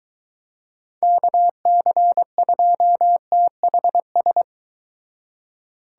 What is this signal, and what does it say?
Morse code, heard as a keyed tone, about 22 wpm, 700 Hz. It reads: KC2THS